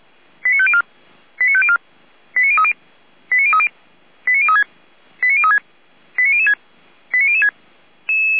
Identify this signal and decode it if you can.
Five-tone selective call (ZVEI), heard as quick runs of five tones, tones of about 70 ms. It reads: 89504 89504 89030 89030 89036 89036 89006 89006